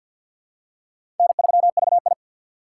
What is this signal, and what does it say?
Morse code, heard as a keyed tone, about 52 wpm, 700 Hz. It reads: N3VI
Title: USA Call Signs - 4 Characters - 3x Character Spacing - 50wpm